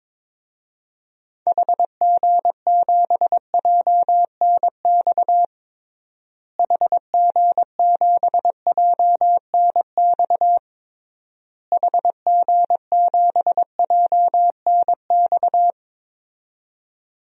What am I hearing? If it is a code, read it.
Morse code, heard as a keyed tone, about 22 wpm, 700 Hz. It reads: HG7JNX HG7JNX HG7JNX
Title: DXCC-100 Call Signs - 6 Characters - Repeated 3x - 22wpm